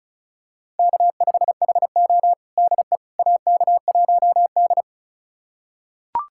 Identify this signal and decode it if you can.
Morse code, heard as a keyed tone, about 34 wpm, 700 Hz. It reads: K5HO DE AK1D
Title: USA Callsign DE Callsign - 4 Characters - 35wpm